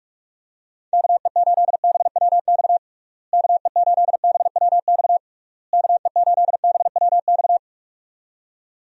Morse code, heard as a keyed tone, about 46 wpm, 700 Hz. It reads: KE8BWX KE8BWX KE8BWX
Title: USA Call Signs - 6 Characters - Repeated 3x - 45wpm